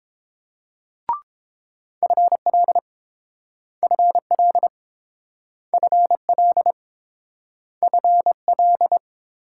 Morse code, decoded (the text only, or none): FL FL FL FL